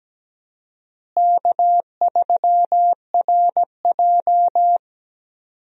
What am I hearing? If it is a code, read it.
Morse code, heard as a keyed tone, about 17 wpm, 700 Hz. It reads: K3RJ
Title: USA Call Signs - 4 Characters - Repeated 3x - 17wpm